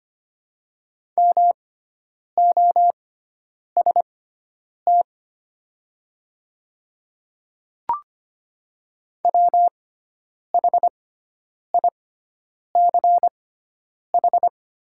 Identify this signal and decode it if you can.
Morse code, heard as a keyed tone, about 25 wpm, spaced as for 9 wpm, 700 Hz. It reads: MOST WHICH
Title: 6x Character Spacing - Top 100 Words - 25wpm